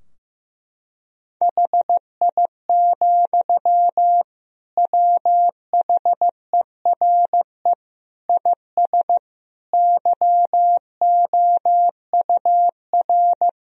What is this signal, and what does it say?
Morse code, heard as a keyed tone, about 15 wpm, 700 Hz. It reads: HI, WHERE IS YOUR
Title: Sentences from Top 1000 Words - Review Part 3 - 15wpm